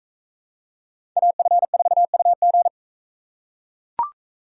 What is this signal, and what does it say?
Morse code, heard as a keyed tone, about 41 wpm, 700 Hz. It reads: AF4UG